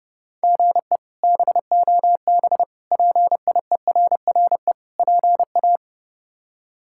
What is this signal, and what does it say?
Morse code, heard as a keyed tone, about 30 wpm, 700 Hz. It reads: GE BOB PIERRE PA
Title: K1USN SST QSO Practice - 30wpm